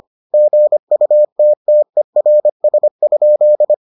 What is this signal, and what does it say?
Morse code, heard as a keyed tone, about 25 wpm, 600 Hz. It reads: GUTTERS?